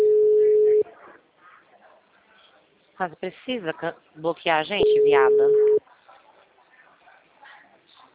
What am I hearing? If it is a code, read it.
Ringback tone (Europe, 425 Hz), with speech.